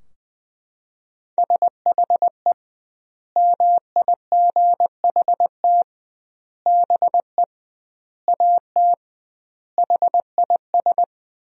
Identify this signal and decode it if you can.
Morse code, heard as a keyed tone, about 21 wpm, 700 Hz. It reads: SHE MIGHT BE AT HIS
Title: 2x Word-Spacing - Sentences from Top 300 Words - 20wpm